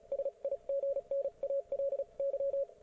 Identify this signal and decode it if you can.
Morse code, heard as a keyed tone, about 35 wpm, spaced as for 30 wpm, 550 Hz. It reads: SIGNALY